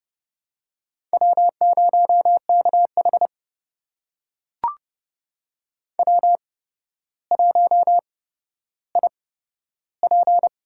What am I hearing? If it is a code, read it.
Morse code, heard as a keyed tone, about 30 wpm, spaced as for 13 wpm, 700 Hz. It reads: W0KH W1IP
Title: USA Call Signs - 4 Characters - 8x Character Spacing - 30wpm